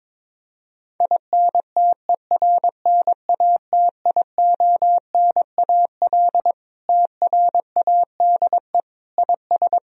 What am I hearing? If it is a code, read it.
Morse code, heard as a keyed tone, about 22 wpm, 700 Hz. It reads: INTERNATIONAL TRADE IS
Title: Sentences from Top 1000 Words - Review Part 4 - 22wpm